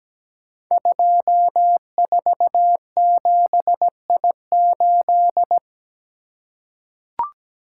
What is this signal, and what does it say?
Morse code, heard as a keyed tone, about 17 wpm, 700 Hz. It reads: 247I8
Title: Code Groups - 5 Letter:Numbers - 17wpm